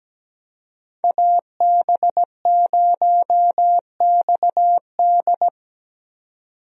Morse code, heard as a keyed tone, about 17 wpm, 700 Hz. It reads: AB0XD
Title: USA Call Signs - 5 Characters - 17wpm